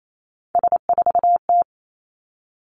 Morse code, heard as a keyed tone, about 28 wpm, 700 Hz. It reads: S4T
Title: Three Challenging Letter-Number Combinations 28wpm